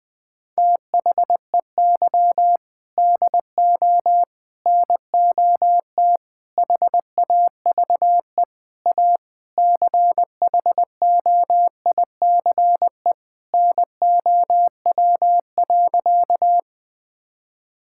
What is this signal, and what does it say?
Morse code, heard as a keyed tone, about 20 wpm, 700 Hz. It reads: THEY DO NOT HAVE A CHOICE NOW.
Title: Sentences with 5-Word Phrases - 20wpm